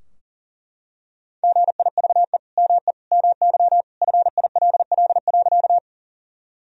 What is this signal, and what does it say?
Morse code, heard as a keyed tone, about 40 wpm, 700 Hz. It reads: GIVE ME MY FILL.
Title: Sentences from Top 1000 Words - Review Part 3 - 40wpm